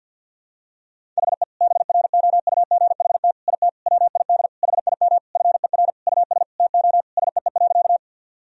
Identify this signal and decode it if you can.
Morse code, heard as a keyed tone, about 49 wpm, 700 Hz. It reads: HE BROUGHT IT WID HIM FER US TO SEE.
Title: Sentences from Top 500 Words and Abbreviations 50wpm